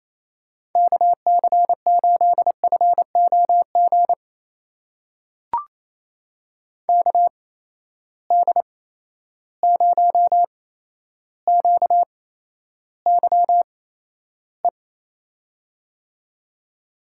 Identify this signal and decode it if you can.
Morse code, heard as a keyed tone, about 27 wpm, spaced as for 12 wpm, 700 Hz. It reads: KC8FOG KD0QYE
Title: USA Call Signs - 6 Characters - 8x Character Spacing - 28wpm